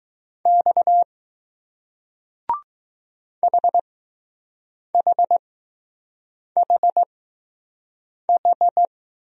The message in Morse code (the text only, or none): X HHHH